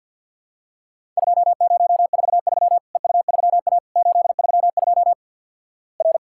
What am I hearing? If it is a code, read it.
Morse code, heard as a keyed tone, about 53 wpm, 700 Hz. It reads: 2043 EU3A 832 R
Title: Worked All Europe DX Contest - QTC Exchange - 50wpm